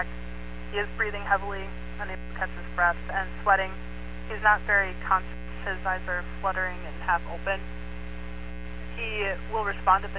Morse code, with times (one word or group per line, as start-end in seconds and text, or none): none